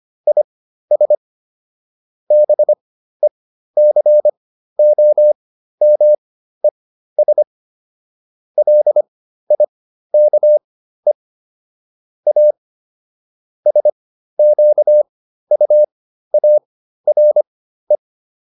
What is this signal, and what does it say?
Morse code, heard as a keyed tone, about 25 wpm, spaced as for 13 wpm, 600 Hz. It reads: IS BECOMES LIKE A SQUARE